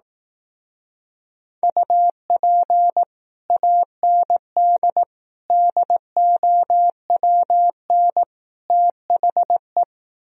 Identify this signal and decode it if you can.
Morse code, heard as a keyed tone, about 18 wpm, 700 Hz. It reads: UP AND DOWN THE
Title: Speed-Racing - Sets of 4 Words 15wpm